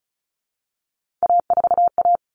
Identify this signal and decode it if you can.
Morse code, heard as a keyed tone, about 34 wpm, 700 Hz. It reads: A4A
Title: All Cut Numbers - 3-Digit 35wpm